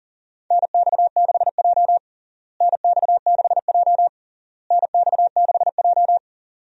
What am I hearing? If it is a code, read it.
Morse code, heard as a keyed tone, about 40 wpm, 700 Hz. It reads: NX6J NX6J NX6J
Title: USA Call Signs - 4 Characters - Repeated 3x - 40wpm